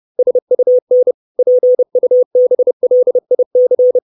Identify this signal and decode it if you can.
Morse code, heard as a keyed tone, about 30 wpm, 500 Hz. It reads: SUN PUBLIC
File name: 632 Words 30WPM